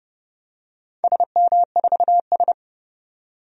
Morse code, heard as a keyed tone, about 31 wpm, 700 Hz. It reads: SM4S
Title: DXCC-100 Call Signs - 4 Characters - 30wpm